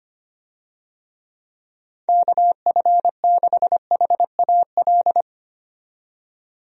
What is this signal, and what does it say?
Morse code, heard as a keyed tone, about 25 wpm, 700 Hz. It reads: KF6HAL
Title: USA Call Signs - 6 Characters - 25wpm